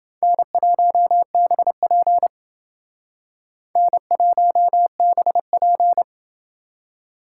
Morse code, heard as a keyed tone, about 28 wpm, 700 Hz. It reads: N1BP N1BP